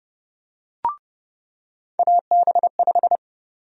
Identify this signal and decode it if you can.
Morse code, heard as a keyed tone, about 32 wpm, 700 Hz. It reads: AB5
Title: USA Call Sign Prefixes - 30wpm